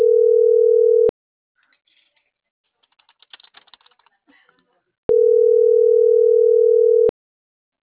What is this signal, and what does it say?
Ringback tone (North America), 440 + 480 Hz.